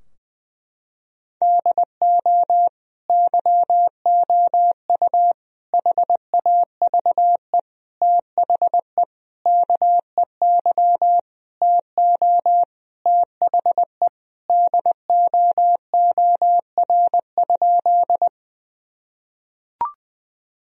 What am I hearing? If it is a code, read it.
Morse code, heard as a keyed tone, about 20 wpm, 700 Hz. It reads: DO YOU HAVE THE KEY TO THE DOOR?
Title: Sentences from Top 1000 Words - Review Part 4 - 20wpm